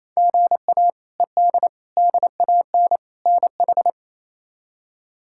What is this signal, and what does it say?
Morse code, heard as a keyed tone, about 28 wpm, 700 Hz. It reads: GA ED DAN NH
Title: K1USN SST QSO Practice - 28wpm